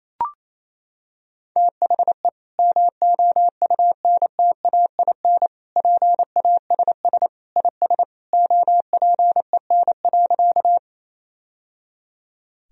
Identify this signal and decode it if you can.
Morse code, heard as a keyed tone, about 28 wpm, 700 Hz. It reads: THE MOUNTAIN PASS IS OPEN.
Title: Sentences from Top 1000 Words - Review Part 1 - 28wpm